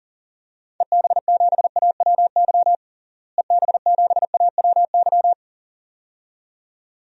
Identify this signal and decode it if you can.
Morse code, heard as a keyed tone, about 40 wpm, 700 Hz. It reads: EB7AWY EB7AWY
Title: DXCC-100 Call Signs - 6 Characters - Repeated 2x - 40wpm